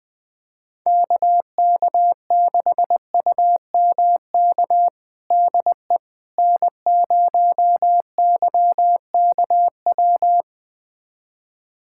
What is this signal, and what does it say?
Morse code, heard as a keyed tone, about 20 wpm, 700 Hz. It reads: KK6UMK DE N0YKW
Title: USA Callsign DE Callsign 20wpm